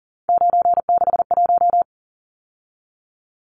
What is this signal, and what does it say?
Morse code, heard as a keyed tone, about 40 wpm, 700 Hz. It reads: 961
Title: Numbers - 3-Digit 40wpm